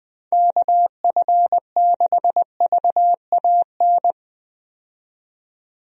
Morse code, heard as a keyed tone, about 21 wpm, 700 Hz. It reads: KF6VAN
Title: USA Call Signs - 6 Characters - 20wpm